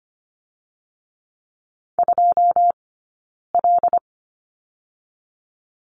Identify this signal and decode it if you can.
Morse code, heard as a keyed tone, about 25 wpm, 700 Hz. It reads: 2 L
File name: course / 25:10wpm / 092 Lesson 16d - Letter c - Two Characters 25wpm